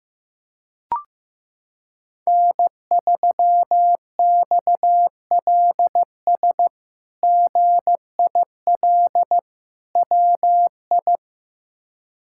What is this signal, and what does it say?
Morse code, heard as a keyed tone, about 16 wpm, 700 Hz. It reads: N3XLS GIL WI
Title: K1USN SST QSO Practice - 15wpm